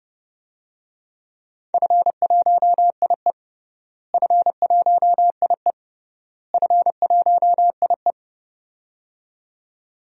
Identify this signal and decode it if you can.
Morse code, heard as a keyed tone, about 32 wpm, 700 Hz. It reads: F1IE F1IE F1IE
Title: DXCC-100 Call Signs - 4 Characters - Repeated 3x - 30wpm